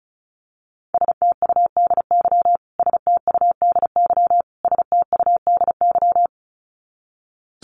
Morse code, heard as a keyed tone, about 34 wpm, 700 Hz. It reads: STUDY STUDY STUDY
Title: Top 500 Words Repeated 35wpm